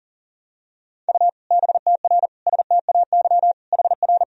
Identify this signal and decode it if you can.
Morse code, heard as a keyed tone, about 40 wpm, 700 Hz. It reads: U BTR STAY HR
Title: Sentences from Top 500 Words and Abbreviations 40wpm